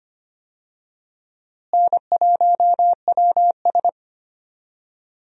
Morse code, heard as a keyed tone, about 25 wpm, 700 Hz. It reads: N1WS
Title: USA Call Signs - 4 Characters - Repeated 3x - 25wpm